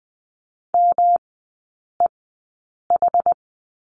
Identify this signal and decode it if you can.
Morse code, heard as a keyed tone, about 19 wpm, spaced as for 8 wpm, 700 Hz. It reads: MEH
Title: 184 Lesson 30e - Letter b - Three Characters 20wpm